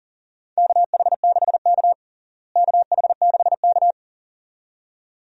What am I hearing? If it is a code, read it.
Morse code, heard as a keyed tone, about 40 wpm, 700 Hz. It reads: KH6K KH6K
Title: DXCC-100 Call Signs - 4 Characters - Repeated 2x - 40wpm